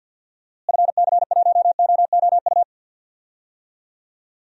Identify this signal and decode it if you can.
Morse code, heard as a keyed tone, about 52 wpm, 700 Hz. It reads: FC1OOU
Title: DXCC-100 Call Signs - 6 Characters - Repeated 3x - 50wpm